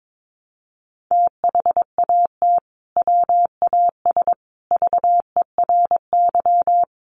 Morse code, heard as a keyed tone, about 22 wpm, 700 Hz. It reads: THAT WAS VERY
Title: Sentences from Top 200 Words 22wpm